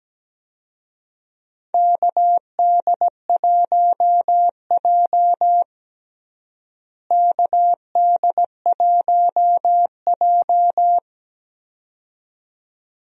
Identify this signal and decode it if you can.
Morse code, heard as a keyed tone, about 17 wpm, 700 Hz. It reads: KD1J KD1J
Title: USA Call Signs - 4 Characters - Repeated 2x - 17wpm